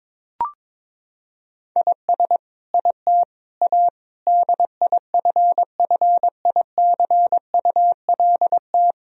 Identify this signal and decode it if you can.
Morse code, heard as a keyed tone, about 22 wpm, 700 Hz. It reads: IS IT A DIFFICULT
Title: Sentences from Top 1000 Words - Review Part 2 - 22wpm